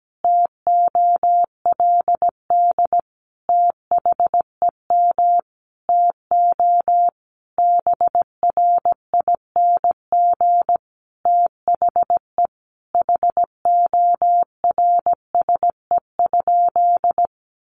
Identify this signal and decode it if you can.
Morse code, heard as a keyed tone, about 17 wpm, 700 Hz. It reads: TOLD THEM TO BRING THE HORSE?
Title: Sentences from Top 500 Words 17wpm